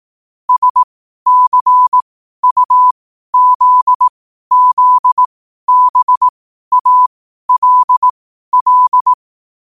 Morse code, heard as a keyed tone, about 18 wpm, spaced as for 12 wpm, 1000 Hz. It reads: SCUZZBALL
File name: qotdfiles / 12wpm